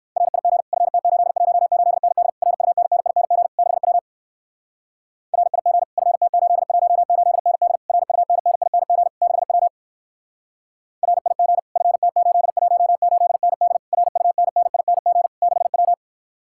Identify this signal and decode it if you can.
Morse code, heard as a keyed tone, about 66 wpm, 700 Hz. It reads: RIG FT818ND RUNNING 6W RIG FT818ND RUNNING 6W RIG FT818ND RUNNING 6W